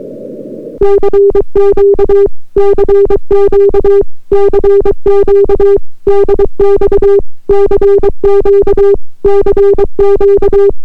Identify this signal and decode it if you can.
Morse code, heard as a keyed tone, about 23 wpm, 400 Hz. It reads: CQ CQ CQ DX CQ CQ